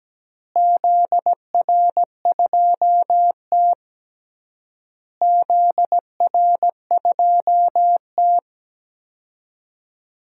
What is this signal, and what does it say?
Morse code, heard as a keyed tone, about 17 wpm, 700 Hz. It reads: ZR2T ZR2T
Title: DXCC-100 Call Signs - 4 Characters - Repeated 3x - 17wpm